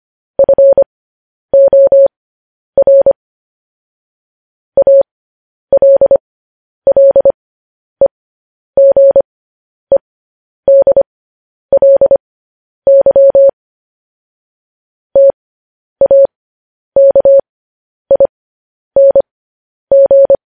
Morse code, heard as a keyed tone, about 25 wpm, spaced as for 10 wpm, 550 Hz. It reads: FOR ALLEGEDLY TAKING